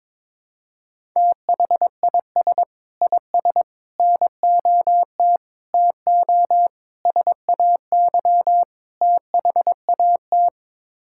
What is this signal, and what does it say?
Morse code, heard as a keyed tone, about 22 wpm, 700 Hz. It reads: THIS IS NOT TO SAY THAT
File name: practice / 22wpm / Sets of 6 Words - 22wpm